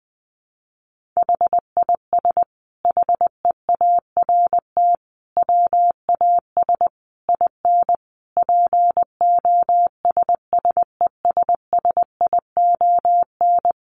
Morse code, heard as a keyed tone, about 20 wpm, 700 Hz. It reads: HIS HEART WAS IN POSSESSION
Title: Sentences from Top 1400 Words 20wpm